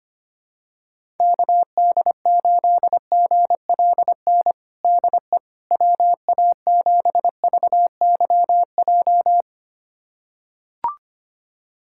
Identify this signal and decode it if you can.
Morse code, heard as a keyed tone, about 25 wpm, 700 Hz. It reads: KD8GLN DE WA7VYJ
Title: USA Callsign DE Callsign 25wpm